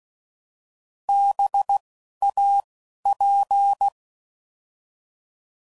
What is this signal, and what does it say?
Morse code, heard as a keyed tone, about 16 wpm, 800 Hz. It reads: B A P